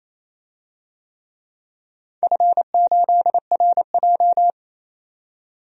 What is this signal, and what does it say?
Morse code, heard as a keyed tone, about 28 wpm, 700 Hz. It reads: F8RJ
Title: DXCC-100 Call Signs - 4 Characters - 28wpm